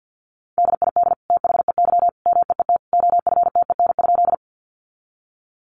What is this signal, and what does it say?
Morse code, heard as a keyed tone, about 50 wpm, 700 Hz. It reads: DID THEY MEET OFTEN?
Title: Sentences from Top 1100 Words 50wpm